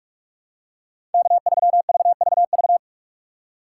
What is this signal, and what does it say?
Morse code, heard as a keyed tone, about 45 wpm, 700 Hz. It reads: K3VVV
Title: USA Call Signs - 2x Character Spacing - 45wpm